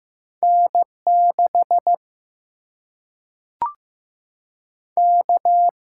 Morse code, heard as a keyed tone, about 15 wpm, 700 Hz. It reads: N 6 K
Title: Single-Letter Single-Number 15wpm